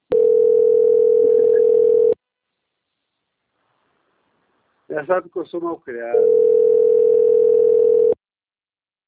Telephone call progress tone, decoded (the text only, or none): ringback tone